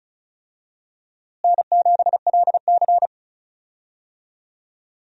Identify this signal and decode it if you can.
Morse code, heard as a keyed tone, about 34 wpm, 700 Hz. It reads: N7LC
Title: USA Call Signs - 4 Characters - Repeated 2x - 35wpm